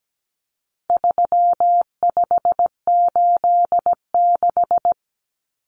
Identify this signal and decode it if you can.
Morse code, heard as a keyed tone, about 17 wpm, 700 Hz. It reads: 3586
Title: Numbers - 4-Digit 17wpm